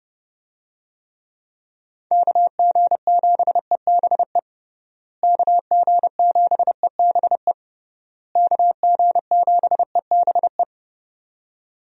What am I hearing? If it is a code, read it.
Morse code, heard as a keyed tone, about 32 wpm, 700 Hz. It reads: KG7EBE KG7EBE KG7EBE
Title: USA Call Signs - 6 Characters - Repeated 3x - 30wpm